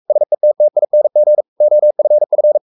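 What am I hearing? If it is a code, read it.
Morse code, heard as a keyed tone, about 44 wpm, 600 Hz. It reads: SETTING OFF
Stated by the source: Morse code at 43 wpm sending Ken Follett's novel Never in English, from CW med Gnister.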